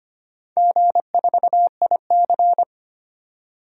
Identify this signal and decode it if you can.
Morse code, heard as a keyed tone, about 25 wpm, 690 Hz. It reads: G4IC
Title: DXCC-100 Call Signs - 4 Characters - 25wpm